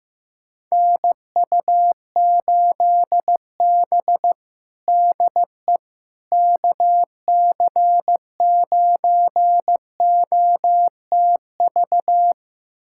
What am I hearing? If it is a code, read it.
Morse code, heard as a keyed tone, about 16 wpm, 700 Hz. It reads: NU8B DE KC9OTV
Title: USA Callsign DE Callsign 15wpm